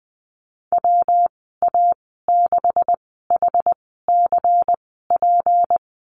Morse code, heard as a keyed tone, about 20 wpm, spaced as for 15 wpm, 700 Hz. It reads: WA6HCP
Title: 161 Lesson 26g - Letter g - Call Signs 20wpm